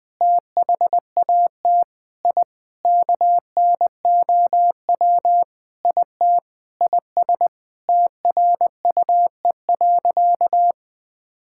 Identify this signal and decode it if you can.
Morse code, heard as a keyed tone, about 20 wpm, 700 Hz. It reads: THAT I KNOW IT IS TRUE.
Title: Sentences with 4-Word Phrases - 20wpm